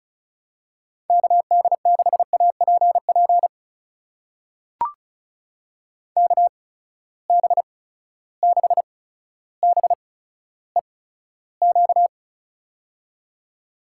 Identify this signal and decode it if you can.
Morse code, heard as a keyed tone, about 36 wpm, spaced as for 15 wpm, 700 Hz. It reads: KD6APP KB6BEQ
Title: USA Call Signs - 6 Characters - 8x Character Spacing - 35wpm